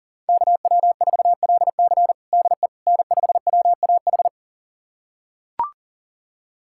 Morse code, heard as a keyed tone, about 37 wpm, 700 Hz. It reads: KW4LC DE N5WAH
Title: USA Callsign DE Callsign - 5 Characters - 40wpm